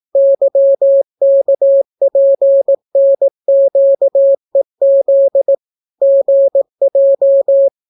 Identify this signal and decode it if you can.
Morse code, heard as a keyed tone, about 18 wpm, 550 Hz. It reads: YKPNQEZ GJ